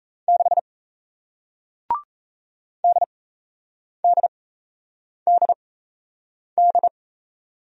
Morse code, heard as a keyed tone, about 38 wpm, spaced as for 8 wpm, 700 Hz. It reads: 6 DDDD